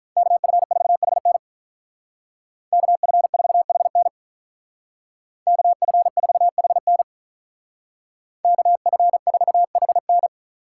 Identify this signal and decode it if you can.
Morse code, heard as a keyed tone, about 43 wpm, 700 Hz. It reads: KF4HN KF4HN KF4HN KF4HN
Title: Speed Racing - USA Call Signs 35wpm